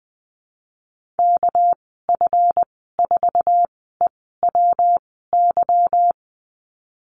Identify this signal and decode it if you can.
Morse code, heard as a keyed tone, about 20 wpm, 700 Hz. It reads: K F 4 E W Y
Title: 238 Lesson 38g - Letter q - Call Signs 20wpm